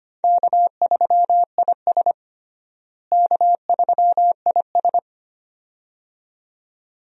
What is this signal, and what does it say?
Morse code, heard as a keyed tone, about 25 wpm, 700 Hz. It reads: K3IS K3IS
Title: USA Call Signs - 4 Characters - Repeated 2x - 25wpm